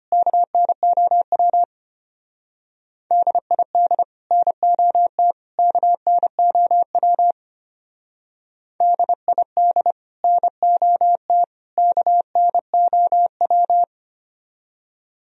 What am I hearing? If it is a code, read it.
Morse code, heard as a keyed tone, about 32 wpm, 700 Hz. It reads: KNOW DID NOT KNOW DID NOT KNOW